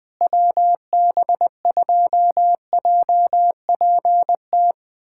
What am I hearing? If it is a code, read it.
Morse code, heard as a keyed tone, about 20 wpm, 700 Hz. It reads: WB2JPT
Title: USA Callsign DE Callsign 20wpm